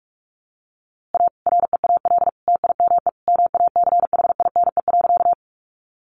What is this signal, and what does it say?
Morse code, heard as a keyed tone, about 44 wpm, 700 Hz. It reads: A REAL TIME MACHINE.
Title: A Sound Of Thunder 45wpm